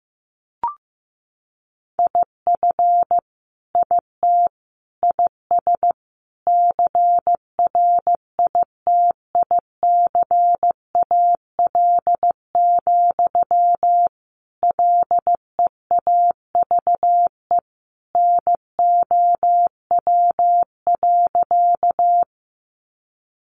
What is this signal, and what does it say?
Morse code, heard as a keyed tone, about 14 wpm, 700 Hz. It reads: IF IT IS CRITICAL, LEAVE NOW.
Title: Sentences from Top 1100 Words 15wpm